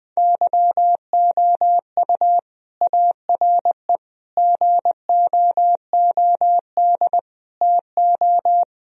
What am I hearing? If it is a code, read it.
Morse code, heard as a keyed tone, about 20 wpm, 700 Hz. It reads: YOU ARE GOOD TO
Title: Sentences from Top 1000 Words - Review Part 1 - 20wpm